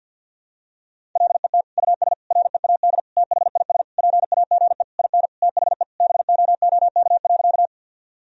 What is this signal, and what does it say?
Morse code, heard as a keyed tone, about 48 wpm, 700 Hz. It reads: LET US READ THIS PAGE IN THE BOOK.
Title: Sentences from Top 1000 Words - Review Part 1 - 50wpm